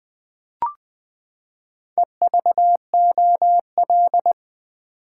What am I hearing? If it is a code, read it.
Morse code, heard as a keyed tone, about 20 wpm, 700 Hz. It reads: EVOL
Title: Build A Word - 4 Letters At A Time - 12-Letter Words - 20wpm